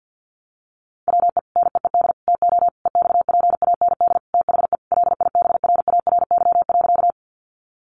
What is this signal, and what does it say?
Morse code, heard as a keyed tone, about 48 wpm, 700 Hz. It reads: WE NEED TO EXPAND THE LIBRARY.
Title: Sentences from Top 1400 Words 50wpm